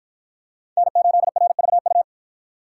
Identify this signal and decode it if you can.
Morse code, heard as a keyed tone, about 53 wpm, 700 Hz. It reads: N8RVU